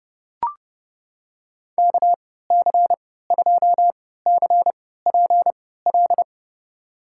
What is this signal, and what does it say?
Morse code, heard as a keyed tone, about 30 wpm, spaced as for 17 wpm, 700 Hz. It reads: KC2CPL
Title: USA Call Signs - 6 Characters - 3x Character Spacing - 30wpm